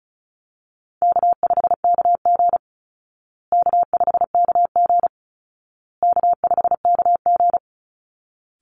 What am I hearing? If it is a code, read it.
Morse code, heard as a keyed tone, about 34 wpm, 700 Hz. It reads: K5KG K5KG K5KG
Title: Easy Call Signs Repeated 3x - 35wpm